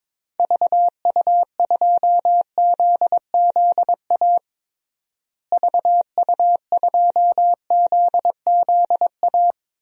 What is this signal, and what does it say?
Morse code, heard as a keyed tone, about 22 wpm, 700 Hz. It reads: VU2ZZA VU2ZZA
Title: DXCC-100 Call Signs - 6 Characters - Repeated 3x - 22wpm